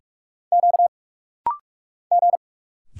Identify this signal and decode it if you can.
Morse code, heard as a keyed tone, about 46 wpm, 690 Hz. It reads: Q G
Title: Single Letters - Rapid-Fire 45wpm